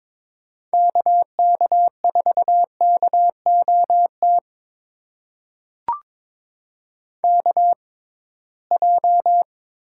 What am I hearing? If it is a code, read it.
Morse code, heard as a keyed tone, about 22 wpm, spaced as for 17 wpm, 700 Hz. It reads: KK4KOT KJ
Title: USA Call Signs - 6 Characters - 6x Character Spacing - 22wpm